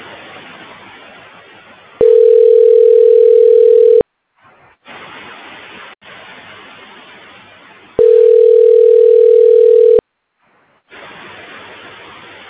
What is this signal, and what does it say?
Ringback tone (North America), 440 + 480 Hz.